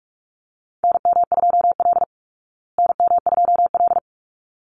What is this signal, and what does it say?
Morse code, heard as a keyed tone, about 44 wpm, 700 Hz. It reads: NM2L NM2L
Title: Call Signs Repeated 3x - 45wpm